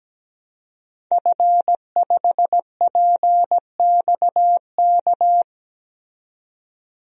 Morse code, heard as a keyed tone, about 17 wpm, 700 Hz. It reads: F5PXK
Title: DXCC-100 Call Signs - 5 Characters - 17wpm